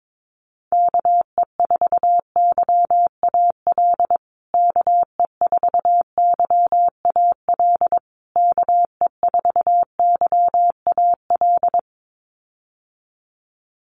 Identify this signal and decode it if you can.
Morse code, heard as a keyed tone, about 22 wpm, 700 Hz. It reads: KE4YAL KE4YAL KE4YAL
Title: USA Call Signs Repeated 3x 22wpm